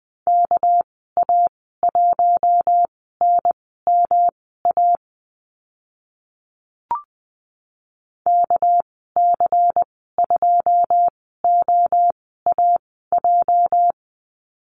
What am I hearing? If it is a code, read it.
Morse code, heard as a keyed tone, about 20 wpm, spaced as for 15 wpm, 700 Hz. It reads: KA1NMA KC2OAJ
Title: 238 Lesson 38g - Letter q - Call Signs 20wpm